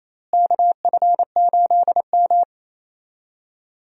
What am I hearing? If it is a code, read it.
Morse code, heard as a keyed tone, about 28 wpm, 700 Hz. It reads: KF8M
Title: USA Call Signs - 4 Characters - Repeated 2x - 28wpm